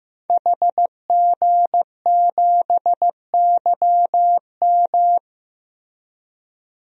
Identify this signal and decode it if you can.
Morse code, heard as a keyed tone, about 15 wpm, 700 Hz. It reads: HG7YM
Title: DXCC-100 Call Signs - 5 Characters - Repeated 2x - 15wpm